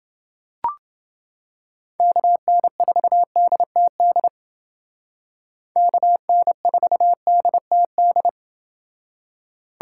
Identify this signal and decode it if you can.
Morse code, heard as a keyed tone, about 30 wpm, 700 Hz. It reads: KN4DTD KN4DTD